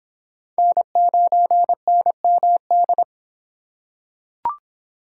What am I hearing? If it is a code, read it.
Morse code, heard as a keyed tone, about 24 wpm, 700 Hz. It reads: N9NMD